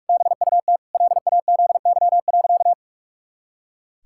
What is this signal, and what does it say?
Morse code, heard as a keyed tone, about 46 wpm, 700 Hz. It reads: BUT LAZY.
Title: Pangram Challenge v2 - 45wpm